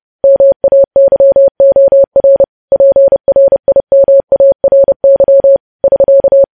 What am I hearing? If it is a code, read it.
Morse code, heard as a keyed tone, about 30 wpm, 550 Hz. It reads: MAYOR PRIMARY <SK>
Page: news-headlines-in-morse-code-at-30-wpm